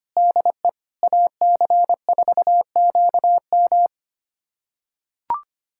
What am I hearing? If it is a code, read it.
Morse code, heard as a keyed tone, about 25 wpm, 700 Hz. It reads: DE AC4QM
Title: USA Callsign DE Callsign - 5 Characters - 25wpm